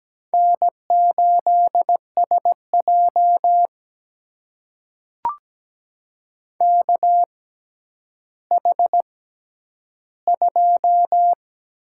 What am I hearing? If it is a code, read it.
Morse code, heard as a keyed tone, about 17 wpm, spaced as for 10 wpm, 700 Hz. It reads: N8SJ KH2
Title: USA Call Signs - 6x Character Spacing - 17wpm